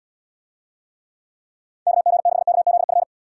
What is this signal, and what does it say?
Morse code, heard as a keyed tone, about 71 wpm, 700 Hz. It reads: KG6KZF